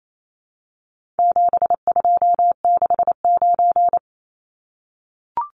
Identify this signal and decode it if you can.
Morse code, heard as a keyed tone, about 28 wpm, 700 Hz. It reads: 7269
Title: Numbers - 4-Digit 28wpm